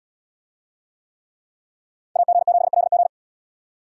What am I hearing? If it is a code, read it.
Morse code, heard as a keyed tone, about 68 wpm, 700 Hz. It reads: AG7LG